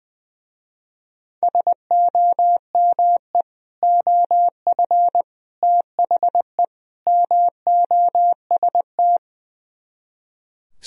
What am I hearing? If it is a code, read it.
Morse code, heard as a keyed tone, about 20 wpm, 700 Hz. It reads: SOME OF THE MOST
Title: Speed-Racing - Sets of 4 Words 20wpm